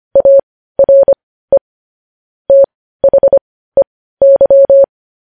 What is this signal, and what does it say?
Morse code, heard as a keyed tone, about 25 wpm, spaced as for 15 wpm, 550 Hz. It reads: ARE THEY
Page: news-headlines-in-morse-code-at-15-wpm